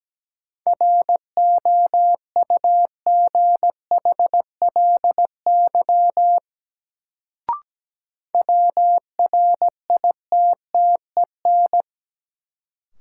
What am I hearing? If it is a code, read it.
Morse code, heard as a keyed tone, about 17 wpm, 700 Hz. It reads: ROUGHLY WRITTEN
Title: Sentences from Top 1500 Words - New Word Drill - 17wpm